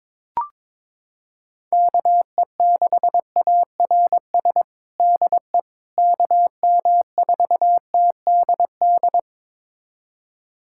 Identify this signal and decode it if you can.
Morse code, heard as a keyed tone, about 22 wpm, 700 Hz. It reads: KE6ARS DE KM4TDD
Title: USA Callsign DE Callsign - 6 Characters - 22wpm